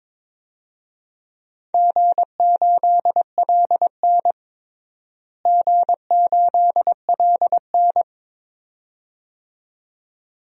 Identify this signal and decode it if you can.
Morse code, heard as a keyed tone, about 23 wpm, 700 Hz. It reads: G8LN G8LN
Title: DXCC-100 Call Signs - 4 Characters - Repeated 2x - 22wpm